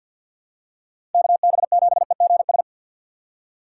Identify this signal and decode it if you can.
Morse code, heard as a keyed tone, about 46 wpm, 700 Hz. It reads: KB7EGS
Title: USA Call Signs - 6x Character Spacing - 50wpm